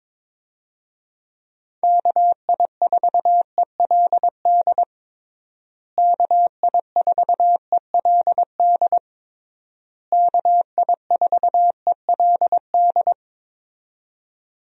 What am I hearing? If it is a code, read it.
Morse code, heard as a keyed tone, about 22 wpm, 700 Hz. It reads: KI4ELD KI4ELD KI4ELD
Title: USA Call Signs - 6 Characters - Repeated 3x - 22wpm